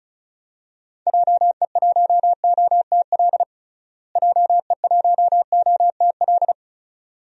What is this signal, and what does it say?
Morse code, heard as a keyed tone, about 34 wpm, 700 Hz. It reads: JE1OTL JE1OTL
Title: DXCC-100 Call Signs - 6 Characters - Repeated 2x - 35wpm